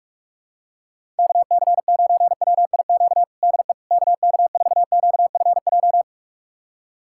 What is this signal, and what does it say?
Morse code, heard as a keyed tone, about 46 wpm, 700 Hz. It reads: KC9WIQ DE KK4QFJ
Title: USA Callsign DE Callsign - 6 Characters - 45wpm